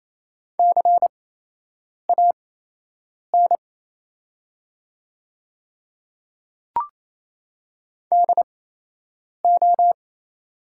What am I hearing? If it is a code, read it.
Morse code, heard as a keyed tone, about 28 wpm, spaced as for 8 wpm, 700 Hz. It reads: CAN DO
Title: 8x Character Spacing - Top 100 Words - 28wpm